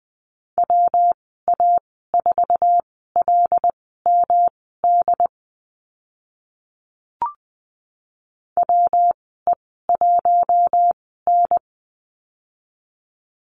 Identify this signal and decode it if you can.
Morse code, heard as a keyed tone, about 20 wpm, spaced as for 14 wpm, 700 Hz. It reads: WA4LMD WE1N
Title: 179 Lesson 29g - Slash - Call Signs 20wpm